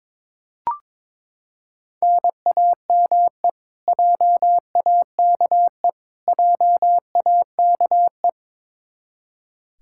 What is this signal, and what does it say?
Morse code, heard as a keyed tone, about 21 wpm, 700 Hz. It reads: NAME JAKE JAKE
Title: Practice QSO - Exchange 1 - Beginning - 22wpm